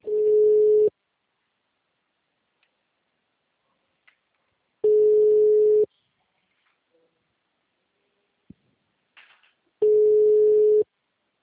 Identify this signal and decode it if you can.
Ringback tone (Europe, 425 Hz).